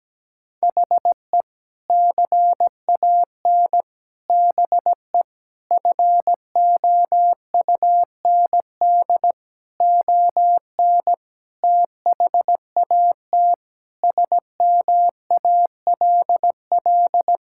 Morse code, heard as a keyed tone, about 17 wpm, 700 Hz. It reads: HE CAN BE FOUND ON THAT SMALL
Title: Sentences from Top 1600 Words 17wpm